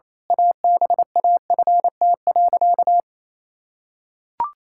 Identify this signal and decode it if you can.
Morse code, heard as a keyed tone, about 27 wpm, 700 Hz. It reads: ABAFT.
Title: Pangram Challenge v2 - 28wpm